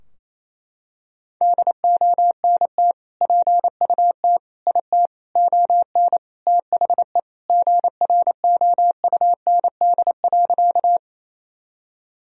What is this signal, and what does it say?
Morse code, heard as a keyed tone, about 28 wpm, 700 Hz. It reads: DONT PUT IT ON THE GROUND.
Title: Sentences from Top 1000 Words - Review Part 2 - 28wpm